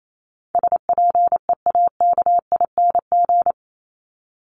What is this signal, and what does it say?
Morse code, heard as a keyed tone, about 28 wpm, 700 Hz. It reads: SPEAKING
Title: ING Suffix - Top 100 Words with Base Words 28wpm